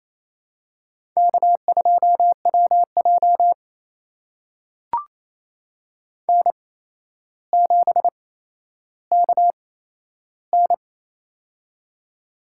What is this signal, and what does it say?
Morse code, heard as a keyed tone, about 28 wpm, spaced as for 12 wpm, 700 Hz. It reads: K2WJ N7KN
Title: USA Call Signs - 4 Characters - 8x Character Spacing - 28wpm